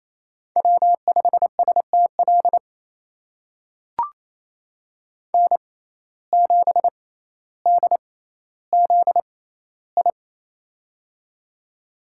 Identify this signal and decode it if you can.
Morse code, heard as a keyed tone, about 28 wpm, spaced as for 14 wpm, 700 Hz. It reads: W5STL N7DZI
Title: USA Call Signs - 5 Characters - 6x Character Spacing - 28wpm